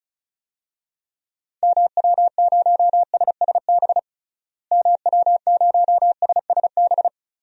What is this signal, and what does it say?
Morse code, heard as a keyed tone, about 36 wpm, 700 Hz. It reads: MW0SSB MW0SSB
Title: DXCC-100 Call Signs - 6 Characters - Repeated 3x - 35wpm